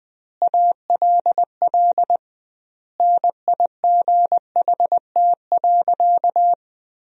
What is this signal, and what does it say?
Morse code, heard as a keyed tone, about 20 wpm, 700 Hz. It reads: ALL NIGHT.
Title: 2x Word-Spacing - Sentences from Top 300 Words - 20wpm